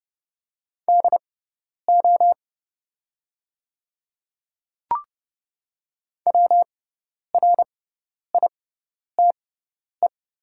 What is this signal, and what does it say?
Morse code, heard as a keyed tone, about 30 wpm, spaced as for 10 wpm, 700 Hz. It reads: DO WRITE
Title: 6x Character Spacing - Top 100 Words - 30wpm